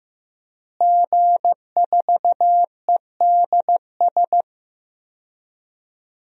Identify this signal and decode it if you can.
Morse code, heard as a keyed tone, about 16 wpm, 700 Hz. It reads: G4EDS
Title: DXCC-100 Call Signs - 5 Characters - Repeated 2x - 15wpm